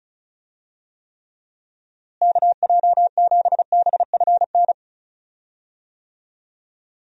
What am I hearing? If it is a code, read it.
Morse code, heard as a keyed tone, about 36 wpm, 700 Hz. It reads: KJ7BFN
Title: USA Call Signs - 6 Characters - Repeated 3x - 35wpm